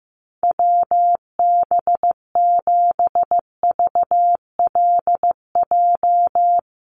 Morse code, heard as a keyed tone, about 15 wpm, 700 Hz. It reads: WB7VLJ
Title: USA Call Signs 15wpm